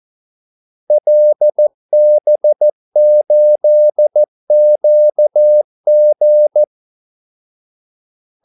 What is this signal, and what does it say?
Morse code, heard as a keyed tone, about 14 wpm, 600 Hz. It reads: LB8QG
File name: Callsigns-14wpm